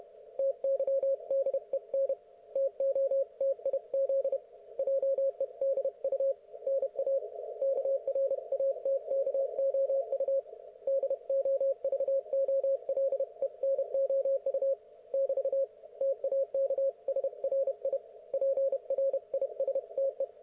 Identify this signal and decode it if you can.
Morse code, heard as a keyed tone, about 31 wpm, 530 Hz. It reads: TYDEN TOTIZ JEDU NA KRATKOU DOVOLENOU <BT> TAK SRI PRISTE
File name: Archiv / CUC107 / ok1fhi cul107